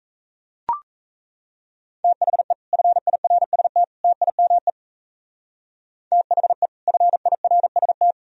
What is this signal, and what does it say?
Morse code, heard as a keyed tone, about 37 wpm, 700 Hz. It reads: THE FIRST TIME THE FIRST